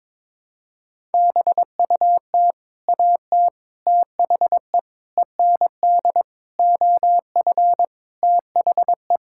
Morse code, heard as a keyed tone, about 22 wpm, 700 Hz. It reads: BUT AT THE END OF THE
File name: practice / 22wpm / Sets of 8 Words - 22wpm